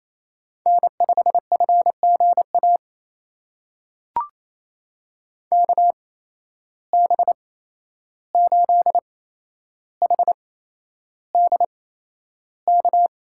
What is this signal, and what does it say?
Morse code, heard as a keyed tone, about 28 wpm, spaced as for 11 wpm, 700 Hz. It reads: N5FGA KB8HDK
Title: USA Call Signs - 8x Character Spacing - 28wpm